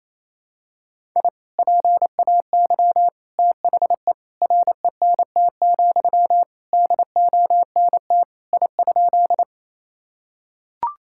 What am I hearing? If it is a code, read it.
Morse code, heard as a keyed tone, about 28 wpm, 700 Hz. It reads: I PAY THE RENT, DONT I?
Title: Sentences from Top 1000 Words - Review Part 5 - 28wpm